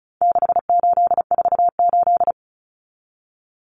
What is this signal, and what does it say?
Morse code, heard as a keyed tone, about 34 wpm, 700 Hz. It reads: 6848
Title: Numbers - 4-Digit 35wpm